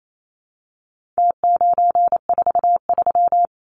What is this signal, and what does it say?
Morse code, heard as a keyed tone, about 28 wpm, 700 Hz. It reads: T943
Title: Numbers - Contest Style - 4-Digit 28wpm